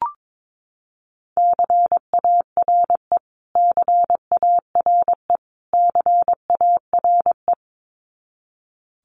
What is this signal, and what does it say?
Morse code, heard as a keyed tone, about 22 wpm, 700 Hz. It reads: CARE CARE CARE
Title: Top 500 Words Repeated 22wpm